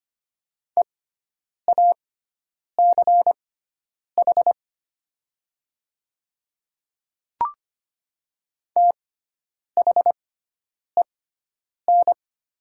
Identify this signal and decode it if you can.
Morse code, heard as a keyed tone, about 25 wpm, spaced as for 9 wpm, 700 Hz. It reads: EACH THEN
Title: 6x Character Spacing - Top 100 Words - 25wpm